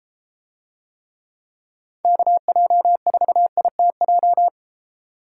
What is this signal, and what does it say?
Morse code, heard as a keyed tone, about 32 wpm, 700 Hz. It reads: KJ4ITJ